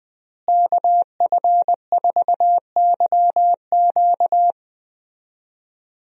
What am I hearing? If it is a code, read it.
Morse code, heard as a keyed tone, about 20 wpm, 700 Hz. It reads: KF4YQ
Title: USA Call Signs - 5 Characters - Repeated 2x - 20wpm